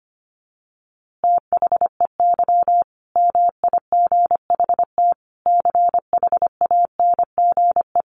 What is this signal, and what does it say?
Morse code, heard as a keyed tone, about 25 wpm, 700 Hz. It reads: THEY MIGHT CHANGE
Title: A Sound Of Thunder 25wpm